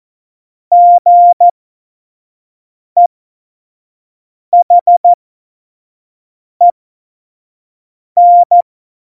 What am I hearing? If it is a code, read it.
Morse code, heard as a keyed tone, about 14 wpm, spaced as for 5 wpm, 700 Hz. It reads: GEHEN
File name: Morsekurs / Witze / Witz 12. Januar 2021, 25 F70